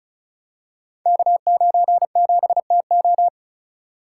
Morse code, heard as a keyed tone, about 35 wpm, 700 Hz. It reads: K97TO
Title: Code Groups - 5 Letter:Numbers - 35wpm